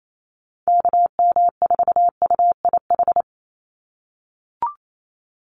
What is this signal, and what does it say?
Morse code, heard as a keyed tone, about 28 wpm, 700 Hz. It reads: KM4UIH
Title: USA Call Signs 28wpm